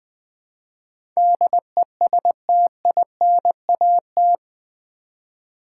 Morse code, heard as a keyed tone, about 20 wpm, 700 Hz. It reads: DESTINAT
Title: Build A Word - 4 Letters At A Time - 11-Letter Words - 20wpm